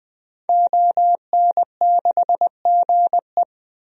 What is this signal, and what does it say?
Morse code, heard as a keyed tone, about 20 wpm, 700 Hz. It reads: ON6GE
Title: DXCC-100 Call Signs - 5 Characters - 20wpm